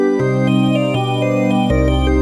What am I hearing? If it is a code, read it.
There is no code.